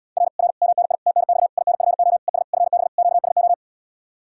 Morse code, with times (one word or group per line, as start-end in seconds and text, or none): 0.2-0.3 R
0.4-0.5 R
0.6-0.9 MNI
1.1-1.5 TNX
1.6-2.2 INFO
2.3-2.4 ES
2.5-2.9 FB
3.0-3.5 QSO